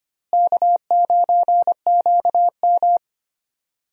Morse code, heard as a keyed tone, about 25 wpm, 700 Hz. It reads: K9QM
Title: USA Call Signs - 4 Characters - Repeated 2x - 25wpm